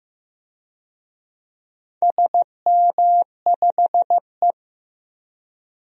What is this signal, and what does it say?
Morse code, heard as a keyed tone, about 15 wpm, 700 Hz. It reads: SM5E
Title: DXCC-100 Call Signs - 4 Characters - Repeated 2x - 15wpm